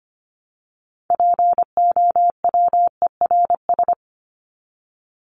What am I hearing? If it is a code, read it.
Morse code, heard as a keyed tone, about 25 wpm, 700 Hz. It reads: POWERS
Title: ES:S Suffix - Top 100 Words with Base Words 25wpm